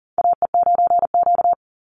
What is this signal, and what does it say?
Morse code, heard as a keyed tone, about 40 wpm, 700 Hz. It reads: AE9Q
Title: NA SOTA Chaser Callsigns 40wpm